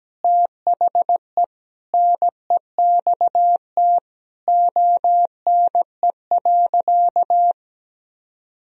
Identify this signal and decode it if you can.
Morse code, heard as a keyed tone, about 17 wpm, 700 Hz. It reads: THE NEXT ONE.
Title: Sentences from Top 1000 Words - Review Part 1 - 17wpm